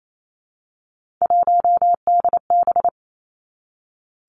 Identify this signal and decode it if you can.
Morse code, heard as a keyed tone, about 28 wpm, 700 Hz. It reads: 1DB
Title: Three Challenging Letter-Number Combinations 28wpm